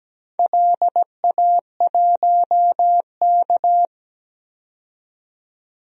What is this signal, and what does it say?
Morse code, heard as a keyed tone, about 17 wpm, 700 Hz. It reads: LA1K
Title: DXCC-100 Call Signs - 4 Characters - 17wpm